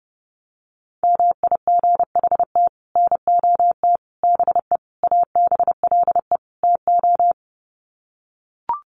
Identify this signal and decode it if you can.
Morse code, heard as a keyed tone, about 30 wpm, 700 Hz. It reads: MIGHT NOT BE ABLE TO
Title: Sets of 5 Words Encore2 30wpm